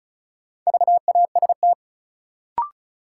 Morse code, heard as a keyed tone, about 36 wpm, 700 Hz. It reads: VAST
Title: Sentences from Top 1300 Words - New Word Drill - 35wpm